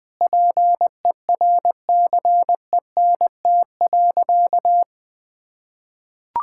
Morse code, heard as a keyed tone, about 21 wpm, 700 Hz. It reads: PERCENT.
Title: Sentences with 6-Word Phrases - Spoken Beginning - 20wpm